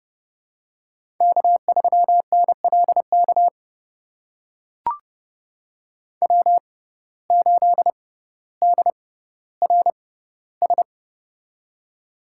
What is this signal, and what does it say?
Morse code, heard as a keyed tone, about 30 wpm, spaced as for 15 wpm, 700 Hz. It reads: K3NLK W8DRS
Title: USA Call Signs - 5 Characters - 6x Character Spacing - 30wpm